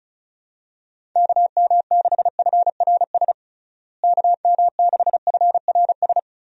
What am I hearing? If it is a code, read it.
Morse code, heard as a keyed tone, about 34 wpm, 700 Hz. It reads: KM6FRS KM6FRS
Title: USA Call Signs - 6 Characters - Repeated 3x - 35wpm